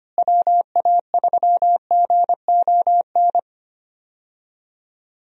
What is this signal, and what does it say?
Morse code, heard as a keyed tone, about 25 wpm, 700 Hz. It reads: WA3GON
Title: USA Call Signs - 6 Characters - Repeated 2x - 25wpm